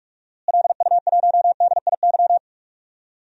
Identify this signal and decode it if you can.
Morse code, heard as a keyed tone, about 41 wpm, 700 Hz. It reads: LU1DIY
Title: DXCC-100 Call Signs - 6 Characters - Repeated 3x - 45wpm